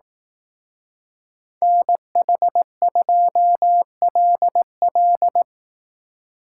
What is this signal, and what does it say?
Morse code, heard as a keyed tone, about 19 wpm, 700 Hz. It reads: NH2LL
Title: Speed Racing - USA 2x2 Call Signs 15wpm